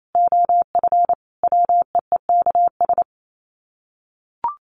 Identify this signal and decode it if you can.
Morse code, heard as a keyed tone, about 28 wpm, 700 Hz. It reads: OF WEEKS
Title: Sets of 5 Words Encore2 28wpm